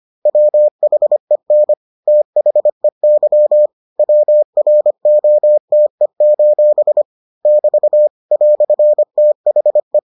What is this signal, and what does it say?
Morse code, heard as a keyed tone, about 25 wpm, 600 Hz. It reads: WHEN THEY WROTE: <BT> "THE